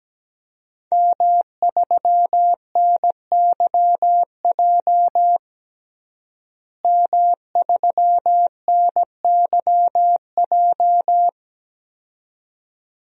Morse code, heard as a keyed tone, about 17 wpm, 700 Hz. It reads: M3NYJ M3NYJ
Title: DXCC-100 Call Signs - 5 Characters - Repeated 2x - 17wpm